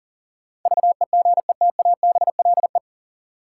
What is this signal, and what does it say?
Morse code, heard as a keyed tone, about 40 wpm, 700 Hz. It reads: VEGETABLE
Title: Sentences from Top 1300 Words - New Word Drill Alt - 40wpm